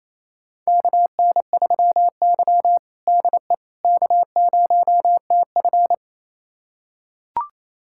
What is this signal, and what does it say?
Morse code, heard as a keyed tone, about 28 wpm, 700 Hz. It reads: KN3Y DE K0TF
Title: USA Callsign DE Callsign - 4 Characters - 28wpm